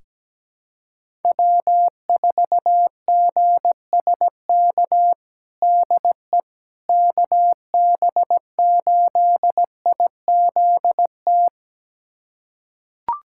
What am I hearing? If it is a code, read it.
Morse code, heard as a keyed tone, about 17 wpm, 700 Hz. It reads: W4GSK DE KB8IZT
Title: USA Callsign DE Callsign 17wpm